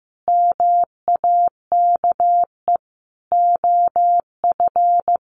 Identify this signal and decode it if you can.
Morse code, heard as a keyed tone, about 15 wpm, 700 Hz. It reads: MAKE OF
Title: Sets of 5 Words 15wpm